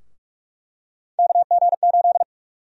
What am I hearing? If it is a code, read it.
Morse code, heard as a keyed tone, about 43 wpm, 700 Hz. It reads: KG8
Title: USA Call Sign Prefixes - Repeated 2x - 45wpm